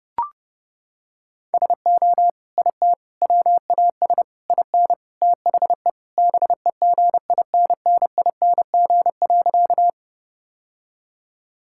Morse code, heard as a keyed tone, about 30 wpm, 700 Hz. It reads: SO IT WAS IN THE BEGINNING.
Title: Sentences from Top 1000 Words - Review Part 5 - 30wpm